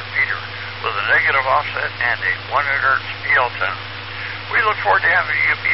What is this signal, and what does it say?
Emergency Alert System (SAME) tones, heard as harsh buzzing bursts, with speech. It is heard from 1.0 to 4.7 s.